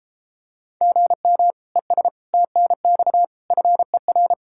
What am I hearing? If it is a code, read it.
Morse code, heard as a keyed tone, about 33 wpm, 700 Hz. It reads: GM ES TNX FER